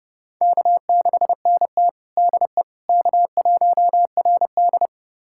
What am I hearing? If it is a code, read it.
Morse code, heard as a keyed tone, about 32 wpm, 700 Hz. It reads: K6NT DE K1RD
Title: USA Callsign DE Callsign - 4 Characters - 30wpm